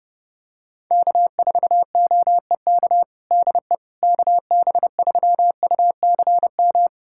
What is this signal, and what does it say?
Morse code, heard as a keyed tone, about 30 wpm, 700 Hz. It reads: K4OEK DE KB3UCM
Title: USA Callsign DE Callsign 30wpm